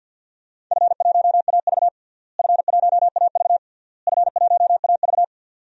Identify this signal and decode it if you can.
Morse code, heard as a keyed tone, about 50 wpm, 700 Hz. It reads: F1AV F1AV F1AV
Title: DXCC-100 Call Signs - 4 Characters - Repeated 3x - 50wpm